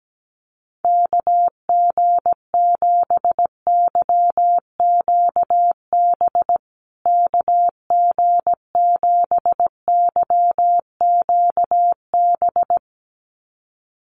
Morse code, heard as a keyed tone, about 17 wpm, 700 Hz. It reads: KG7YQB KG7YQB
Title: USA Call Signs Repeated 2x 17wpm